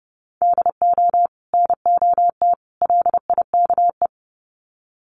Morse code, heard as a keyed tone, about 30 wpm, 700 Hz. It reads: DO NOT LIKE
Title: Sets of 3 Words 30wpm